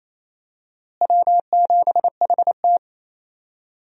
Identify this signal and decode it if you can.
Morse code, heard as a keyed tone, about 28 wpm, 700 Hz. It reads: W7HT
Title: USA Call Signs - 4 Characters - Repeated 3x - 28wpm